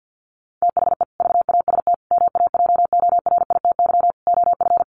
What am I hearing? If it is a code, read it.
Morse code, heard as a keyed tone, about 50 wpm, 700 Hz. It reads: THE VAST MAJORITY OF
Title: Sets of 4 Words 50wpm